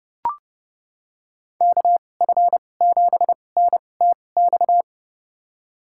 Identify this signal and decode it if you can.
Morse code, heard as a keyed tone, about 30 wpm, spaced as for 22 wpm, 700 Hz. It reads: KF7NTX
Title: USA Call Signs - 6 Characters - 2x Character Spacing - 30wpm